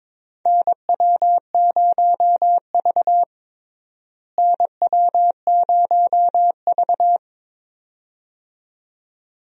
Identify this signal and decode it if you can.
Morse code, heard as a keyed tone, about 21 wpm, 700 Hz. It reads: NW0V NW0V
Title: USA Call Signs - 4 Characters - Repeated 2x - 22wpm